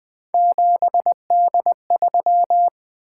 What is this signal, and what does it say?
Morse code, heard as a keyed tone, about 20 wpm, 700 Hz. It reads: 7D3
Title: Three Letter-Number Combinations - 20wpm